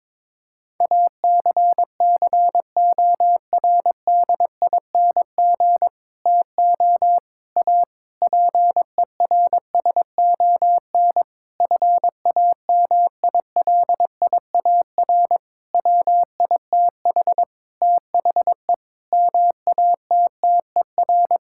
Morse code, heard as a keyed tone, about 22 wpm, 700 Hz. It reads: ACCORDING TO A PERSON FAMILIAR WITH THE MATTER
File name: Sentences with 7-Word Phrases - 22wpm